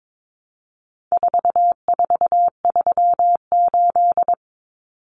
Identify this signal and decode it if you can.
Morse code, heard as a keyed tone, about 22 wpm, 700 Hz. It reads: 4438
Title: Numbers - 4-Digit 22wpm